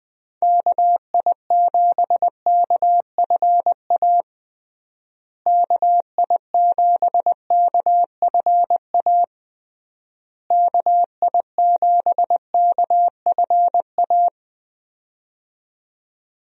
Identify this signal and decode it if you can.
Morse code, heard as a keyed tone, about 20 wpm, 700 Hz. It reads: KI7KFA KI7KFA KI7KFA
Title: USA Call Signs - 6 Characters - Repeated 3x - 20wpm